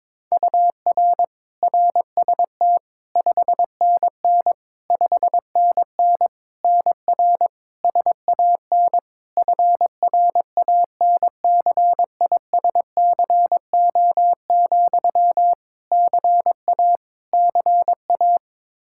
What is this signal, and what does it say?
Morse code, heard as a keyed tone, about 22 wpm, 700 Hz. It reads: UR RST 5NN 5NN NR SAN FRANCISCO, CA CA
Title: Practice QSO - Exchange 1 - Beginning - 22wpm